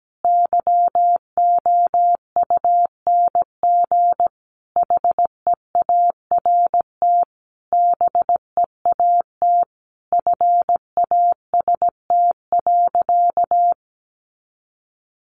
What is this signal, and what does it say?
Morse code, heard as a keyed tone, about 17 wpm, 700 Hz. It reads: YOUNG HEART BEAT FAST.
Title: Sentences from Top 1300 Words 17wpm